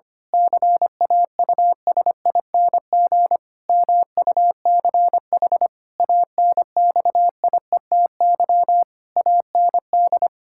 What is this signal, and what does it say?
Morse code, heard as a keyed tone, about 25 wpm, 700 Hz. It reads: CAUSING MUCH ANXIETY AND
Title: Pangram Challenge v2 - 25wpm